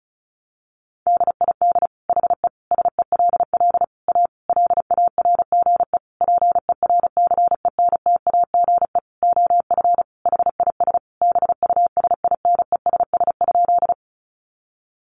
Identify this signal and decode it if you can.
Morse code, heard as a keyed tone, about 34 wpm, 700 Hz. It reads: DID HE SELL A LARGE PERCENTAGE OF HIS BUSINESS?
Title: Sentences from Top 1300 Words 35wpm